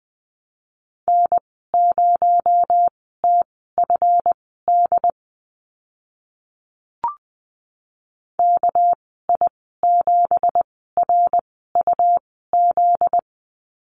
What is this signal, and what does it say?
Morse code, heard as a keyed tone, about 20 wpm, spaced as for 14 wpm, 700 Hz. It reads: N0TFD KI7RUZ
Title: 245 Lesson 39g - Letter z - Call Signs 20wpm